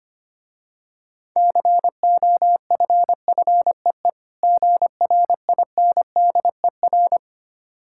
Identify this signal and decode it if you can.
Morse code, heard as a keyed tone, about 25 wpm, 700 Hz. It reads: COFFEE GRINDER
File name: Easy Words - Speed-Racing with Hints - 25wpm